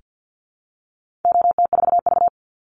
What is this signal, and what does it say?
Morse code, heard as a keyed tone, about 48 wpm, 700 Hz. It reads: OT4V
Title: Call Signs Repeated 3x - 50wpm